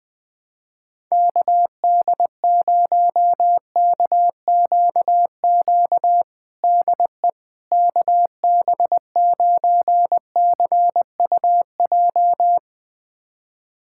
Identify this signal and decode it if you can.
Morse code, heard as a keyed tone, about 21 wpm, 700 Hz. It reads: KD0KQQ DE KB9CUJ
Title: USA Callsign DE Callsign - 6 Characters - 20wpm